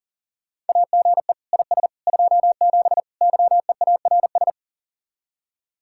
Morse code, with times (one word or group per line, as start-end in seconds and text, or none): 0.7-1.3 AGE
1.5-1.9 IS
2.1-3.0 27
3.2-4.5 YEARS